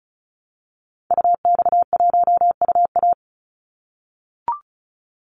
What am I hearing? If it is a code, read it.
Morse code, heard as a keyed tone, about 34 wpm, 700 Hz. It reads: UX1UA
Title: Call Signs - 35wpm